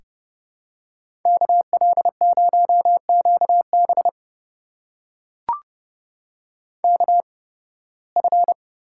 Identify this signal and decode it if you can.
Morse code, heard as a keyed tone, about 31 wpm, spaced as for 20 wpm, 700 Hz. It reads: KL0QB KF
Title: USA Call Signs - 5 Characters - 8x Character Spacing - 30wpm